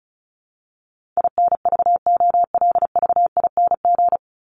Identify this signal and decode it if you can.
Morse code, heard as a keyed tone, about 34 wpm, 700 Hz. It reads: INVOLVING
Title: ING Suffix - Top 100 Words with Base Words 35wpm